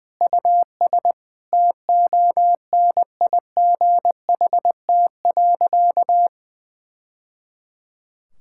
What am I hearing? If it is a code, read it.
Morse code, heard as a keyed tone, about 21 wpm, 700 Hz. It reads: US TONIGHT.
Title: Sentences from Top 1000 Words - Review Part 5 - 20wpm